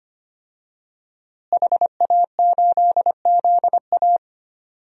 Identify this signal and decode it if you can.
Morse code, heard as a keyed tone, about 25 wpm, 700 Hz. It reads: HA8ZA
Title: DXCC-100 Call Signs - 5 Characters - Repeated 3x - 25wpm